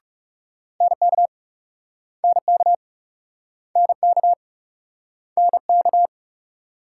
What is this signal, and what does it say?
Morse code, heard as a keyed tone, about 40 wpm, 700 Hz. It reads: NK NK NK NK